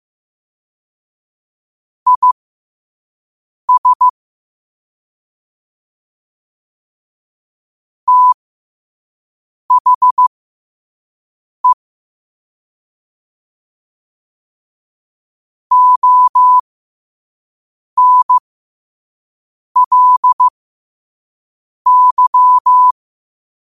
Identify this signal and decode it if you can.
Morse code, heard as a keyed tone, about 15 wpm, spaced as for 5 wpm, 1000 Hz. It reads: IS THE ONLY
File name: 2024-10-07-5wpm